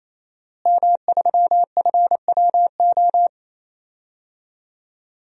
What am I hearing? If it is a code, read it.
Morse code, heard as a keyed tone, about 28 wpm, 700 Hz. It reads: M3FWO
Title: DXCC-100 Call Signs - 5 Characters - Repeated 3x - 28wpm